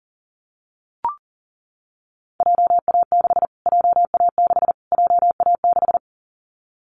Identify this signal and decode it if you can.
Morse code, heard as a keyed tone, about 40 wpm, 700 Hz. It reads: JA6 JA6 JA6
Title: Callsign Prefixes for Top 25 DXCC - Repeated 3x - 40wpm